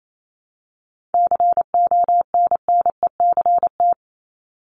Morse code, heard as a keyed tone, about 28 wpm, 700 Hz. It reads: CONNECT
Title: Top 1251-1500 Words 28wpm